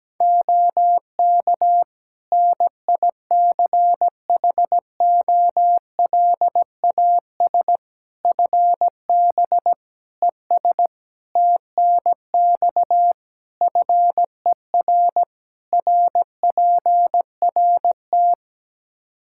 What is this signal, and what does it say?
Morse code, heard as a keyed tone, about 17 wpm, 700 Hz. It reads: OK NICHOLAS FB ES TNX FER RPRT
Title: Practice QSO - Exchange 2 - Info - 17wpm